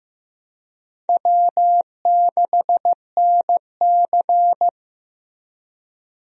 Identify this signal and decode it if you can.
Morse code, heard as a keyed tone, about 15 wpm, 700 Hz. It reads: W6NC
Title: USA Call Signs - 4 Characters - Repeated 3x - 15wpm